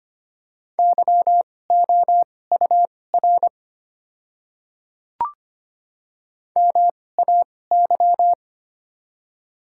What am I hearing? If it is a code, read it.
Morse code, heard as a keyed tone, about 25 wpm, spaced as for 18 wpm, 700 Hz. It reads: YOUR MAY
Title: 2x Character Spacing - Top 200 Words - 25wpm